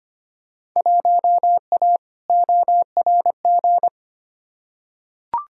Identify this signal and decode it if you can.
Morse code, heard as a keyed tone, about 25 wpm, 700 Hz. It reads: 1A ORG
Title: Field Day Exchanges Encore - 25wpm